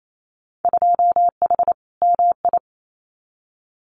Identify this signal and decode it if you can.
Morse code, heard as a keyed tone, about 28 wpm, 700 Hz. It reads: 2H MI
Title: Winter Field Day Exchanges Encore 28wpm